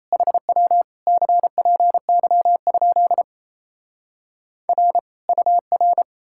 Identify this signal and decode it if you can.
Morse code, heard as a keyed tone, about 37 wpm, 700 Hz. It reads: HW CPY? R UR